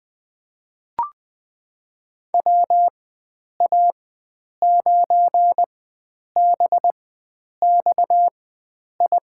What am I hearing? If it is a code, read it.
Morse code, heard as a keyed tone, about 21 wpm, spaced as for 9 wpm, 700 Hz. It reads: WA9BXI
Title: USA Call Signs - 6 Characters - 4x Character Spacing - 20wpm